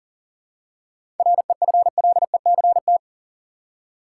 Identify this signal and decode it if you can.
Morse code, heard as a keyed tone, about 40 wpm, 700 Hz. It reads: REFLECT
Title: Seven Letter Words - 40wpm